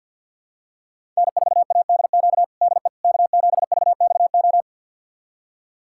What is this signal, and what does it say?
Morse code, heard as a keyed tone, about 51 wpm, 700 Hz. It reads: N4ADQ DE K7VXO
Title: USA Callsign DE Callsign - 5 Characters - 50wpm